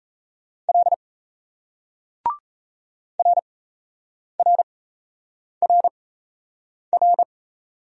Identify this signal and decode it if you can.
Morse code, heard as a keyed tone, about 38 wpm, spaced as for 8 wpm, 700 Hz. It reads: L RRRR